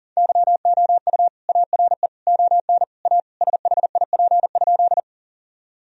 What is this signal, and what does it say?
Morse code, heard as a keyed tone, about 40 wpm, 700 Hz. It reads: YOU ARE ON A SHIP?
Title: Sentences from Top 1000 Words - Review Part 2 - 40wpm